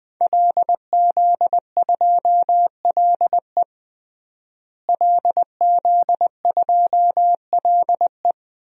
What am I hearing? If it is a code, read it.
Morse code, heard as a keyed tone, about 20 wpm, 700 Hz. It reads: LZ2LE LZ2LE
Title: DXCC-100 Call Signs - 5 Characters - Repeated 2x - 20wpm